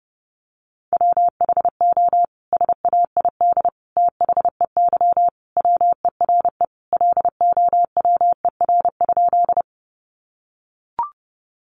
Morse code, heard as a keyed tone, about 28 wpm, 700 Hz. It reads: WHO SAID THEY WERE LOWER?
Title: Sentences from Top 700 Words 30wpm